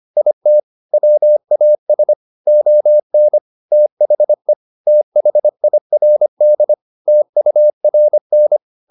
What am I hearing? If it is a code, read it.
Morse code, heard as a keyed tone, about 25 wpm, 600 Hz. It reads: IT WAS ON THE THIRD TURN